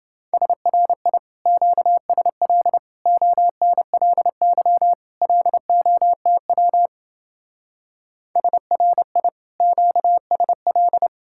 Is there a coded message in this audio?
Yes: Morse code from 0.3 to 11.1 s, heard as a keyed tone, about 28 wpm, 700 Hz. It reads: SRI QSL ONLY LOTW SRI QSL